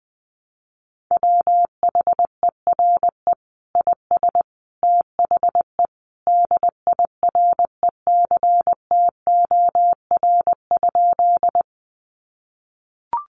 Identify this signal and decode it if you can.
Morse code, heard as a keyed tone, about 20 wpm, 700 Hz. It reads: WHERE IS THE DIRECTOR?
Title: Sentences from Top 1300 Words 20wpm